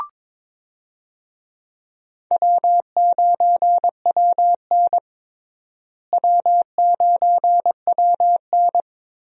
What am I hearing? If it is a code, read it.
Morse code, heard as a keyed tone, about 21 wpm, 700 Hz. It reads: W9WN W9WN
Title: USA Call Signs - 4 Characters - Repeated 3x - 22wpm